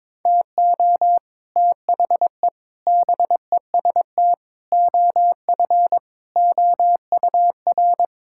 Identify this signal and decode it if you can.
Morse code, heard as a keyed tone, about 21 wpm, 700 Hz. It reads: TO THE BEST OF OUR
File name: practice / 22wpm / Sets of 8 Words - 22wpm